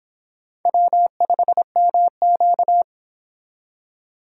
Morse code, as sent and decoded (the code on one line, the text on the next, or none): .-- ..... -- --.-
W5MQ